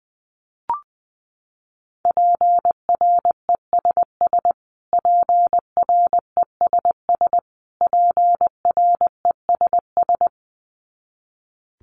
Morse code, heard as a keyed tone, about 20 wpm, 700 Hz. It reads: PRESS PRESS PRESS
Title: Top 500 Words Repeated 20wpm